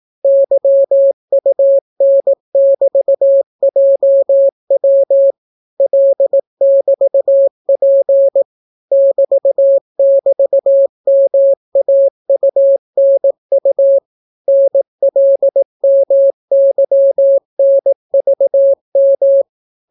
Morse code, heard as a keyed tone, about 18 wpm, 550 Hz. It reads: YUN<BT>JW L<BT>P <BT><BT>MAUNU NLMYNVM